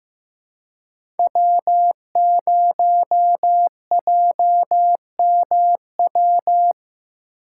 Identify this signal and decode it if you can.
Morse code, heard as a keyed tone, about 15 wpm, 700 Hz. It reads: W0JMW
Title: USA Call Signs - 5 Characters - 6x Character Spacing - 15wpm